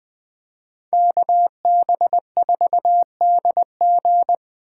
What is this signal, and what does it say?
Morse code, heard as a keyed tone, about 20 wpm, 700 Hz. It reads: KB4DG
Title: Speed Racing - USA 2x2 Call Signs 17wpm